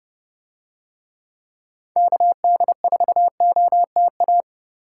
Morse code, heard as a keyed tone, about 28 wpm, 700 Hz. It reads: KD4OTA